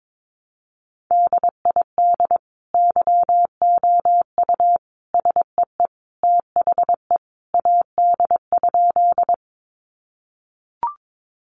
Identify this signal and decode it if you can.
Morse code, heard as a keyed tone, about 22 wpm, 700 Hz. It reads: DID YOU SEE THE AD?
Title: Sentences from Top 1300 Words 22wpm